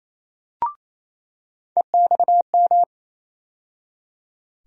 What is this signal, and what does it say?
Morse code, heard as a keyed tone, about 28 wpm, 700 Hz. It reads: EXM
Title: USA Call Sign Postfixes - 3 Letters 28wpm